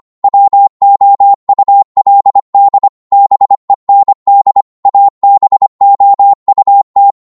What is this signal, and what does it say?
Morse code, heard as a keyed tone, about 25 wpm, 800 Hz. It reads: WOULD BEND ABOUT
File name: Morse / psg-25wpm